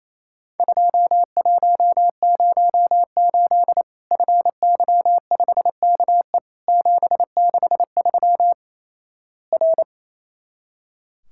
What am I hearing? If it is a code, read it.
Morse code, heard as a keyed tone, about 28 wpm, 700 Hz. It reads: 2108 FY5KE 763 R
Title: Worked All Europe DX Contest - QTC Exchange - 28wpm